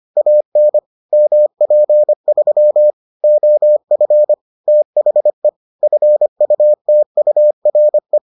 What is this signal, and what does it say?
Morse code, heard as a keyed tone, about 25 wpm, 600 Hz. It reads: AN MP3 OF THE FUTURE